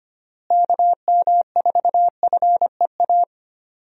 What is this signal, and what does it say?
Morse code, heard as a keyed tone, about 25 wpm, 700 Hz. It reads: KM4FEA
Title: USA Call Signs - 6 Characters - Repeated 3x - 25wpm